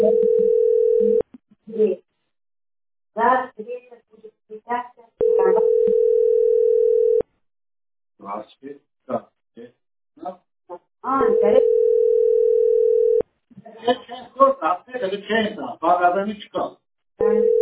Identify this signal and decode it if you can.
Ringback tone (North America), 440 + 480 Hz, with speech.
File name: 06fcaa6f-0f91-4d3f-a866